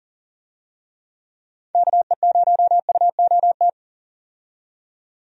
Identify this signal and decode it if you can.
Morse code, heard as a keyed tone, about 40 wpm, 700 Hz. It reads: KE0UOT
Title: USA Call Signs - 6 Characters - 40wpm